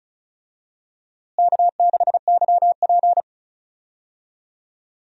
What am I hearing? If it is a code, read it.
Morse code, heard as a keyed tone, about 36 wpm, 700 Hz. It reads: K6YP
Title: USA Call Signs - 4 Characters - Repeated 2x - 35wpm